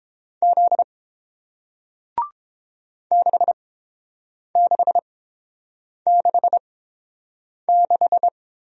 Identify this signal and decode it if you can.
Morse code, heard as a keyed tone, about 30 wpm, spaced as for 7 wpm, 700 Hz. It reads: Z 6666